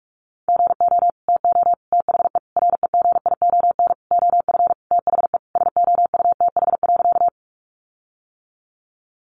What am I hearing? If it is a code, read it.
Morse code, heard as a keyed tone, about 44 wpm, 700 Hz. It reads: GO TO THE REGION OF THE SOUTH.
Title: Sentences from Top 1500 Words 45wpm